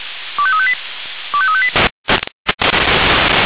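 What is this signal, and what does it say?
Five-tone selective call (ZVEI), heard as quick runs of five tones, tones of about 70 ms. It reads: 25358 25358
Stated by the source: ZVEI five-tone selective call received on SDR.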